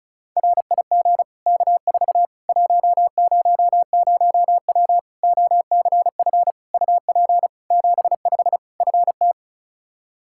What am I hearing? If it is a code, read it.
Morse code, heard as a keyed tone, about 34 wpm, 700 Hz. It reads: RIG K4 100W OCF UP 75 FT
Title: Practice QSO - Exchange 2 - Info - 35wpm